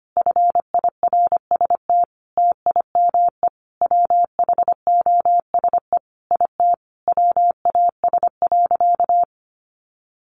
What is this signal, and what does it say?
Morse code, heard as a keyed tone, about 25 wpm, 700 Hz. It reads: FIRST TIME WHOSE IT WAS.
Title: Sentences from Top 1200 Words 25wpm